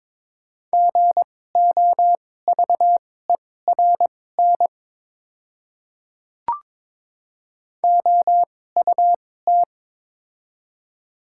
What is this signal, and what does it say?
Morse code, heard as a keyed tone, about 22 wpm, spaced as for 16 wpm, 700 Hz. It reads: GOVERN OUT
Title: 2x Character Spacing - Top 500 Words - 22wpm